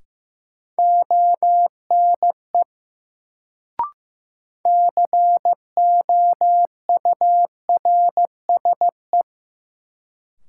Morse code, heard as a keyed tone, about 15 wpm, 700 Hz. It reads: ONE COURSE
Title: Sets of 2 Words - New Word Drill - 15wpm